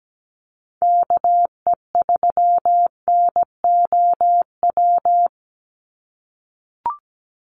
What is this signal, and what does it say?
Morse code, heard as a keyed tone, about 17 wpm, 700 Hz. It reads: KE3NOW
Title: USA Call Signs 17wpm